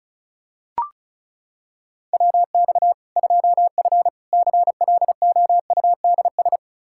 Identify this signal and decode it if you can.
Morse code, heard as a keyed tone, about 36 wpm, 700 Hz. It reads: WX 2F CLOUDS
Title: Practice QSO - Exchange 2 - Info - 35wpm